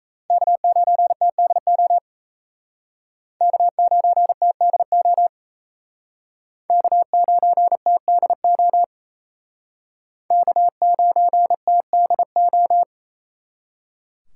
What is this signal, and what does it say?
Morse code, heard as a keyed tone, about 35 wpm, 700 Hz. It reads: K9TDO K9TDO K9TDO K9TDO
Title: Speed Racing - USA 1x3 Call Signs 28wpm